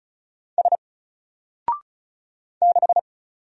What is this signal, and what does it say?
Morse code, heard as a keyed tone, about 35 wpm, 700 Hz. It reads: S 6
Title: Single Character (Letter:Number:Punctuation:Prosign) 35wpm